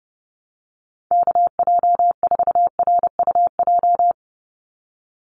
Morse code, heard as a keyed tone, about 30 wpm, 700 Hz. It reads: KJ4RUJ
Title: USA Call Signs 30wpm